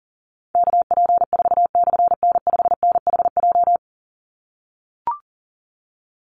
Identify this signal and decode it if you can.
Morse code, heard as a keyed tone, about 40 wpm, 700 Hz. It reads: KP4/N5NHJ
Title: Call Signs - 40wpm